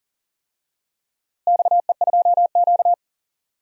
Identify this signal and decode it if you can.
Morse code, heard as a keyed tone, about 40 wpm, 700 Hz. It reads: XE2Q
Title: DXCC-100 Call Signs - 4 Characters - Repeated 3x - 40wpm